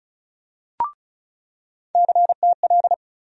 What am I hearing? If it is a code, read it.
Morse code, heard as a keyed tone, about 36 wpm, 700 Hz. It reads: CTL
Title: USA Call Sign Postfixes - 3 Letters 35wpm